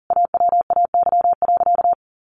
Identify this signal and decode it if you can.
Morse code, heard as a keyed tone, about 40 wpm, 700 Hz. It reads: AWAY.
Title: Sentences from Top 1100 Words 40wpm